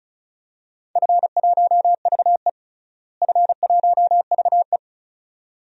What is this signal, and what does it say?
Morse code, heard as a keyed tone, about 34 wpm, 700 Hz. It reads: F1VE F1VE
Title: DXCC-100 Call Signs - 4 Characters - Repeated 2x - 35wpm